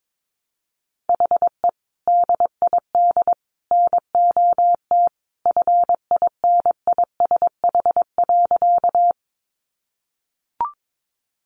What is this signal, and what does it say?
Morse code, heard as a keyed tone, about 22 wpm, 700 Hz. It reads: HE DID NOT FINISH.
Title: Sentences from Top 1200 Words 22wpm